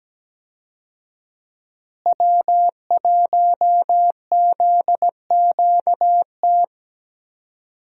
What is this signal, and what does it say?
Morse code, heard as a keyed tone, about 17 wpm, 700 Hz. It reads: W1ZQT
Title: USA Call Signs - 5 Characters - Repeated 2x - 17wpm